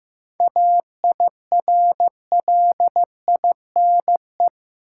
Morse code, heard as a keyed tone, about 15 wpm, 700 Hz. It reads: AIRLINE
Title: Seven Letter Words - 15wpm